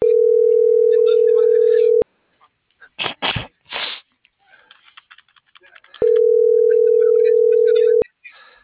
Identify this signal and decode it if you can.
Ringback tone (North America), 440 + 480 Hz.